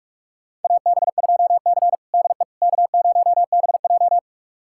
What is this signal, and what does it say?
Morse code, heard as a keyed tone, about 46 wpm, 700 Hz. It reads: AB2C DE K0BJ
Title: USA Callsign DE Callsign - 4 Characters - 45wpm